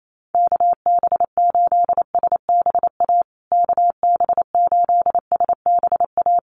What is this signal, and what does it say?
Morse code, heard as a keyed tone, about 28 wpm, 700 Hz. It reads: KB8SBA KB8SBA
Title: USA Call Signs Repeated 3x 28wpm